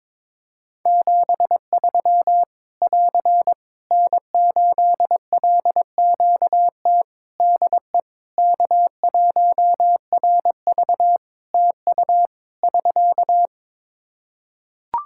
Morse code, heard as a keyed tone, about 22 wpm, 700 Hz. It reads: 73 <AR> N8LQT DE K1RV TU <SK>
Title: K1USN SST QSO Practice 22wpm